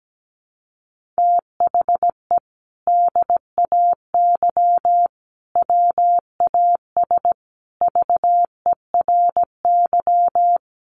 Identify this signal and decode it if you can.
Morse code, heard as a keyed tone, about 17 wpm, 700 Hz. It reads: THE DAY WAS VERY
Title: Sentences from Top 1500 Words 17wpm